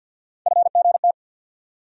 Morse code, heard as a keyed tone, about 51 wpm, 690 Hz. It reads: FGT
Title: USA Call Sign Postfixes - 3 Letters 50wpm